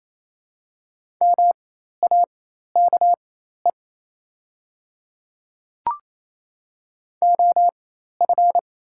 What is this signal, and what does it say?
Morse code, heard as a keyed tone, about 28 wpm, spaced as for 13 wpm, 700 Hz. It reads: MAKE OF
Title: 4x Character Spacing - Top 100 Words - 28wpm